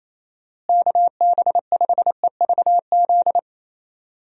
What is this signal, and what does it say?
Morse code, heard as a keyed tone, about 28 wpm, 700 Hz. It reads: KB5EVZ
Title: USA Call Signs - 6 Characters - Repeated 2x - 28wpm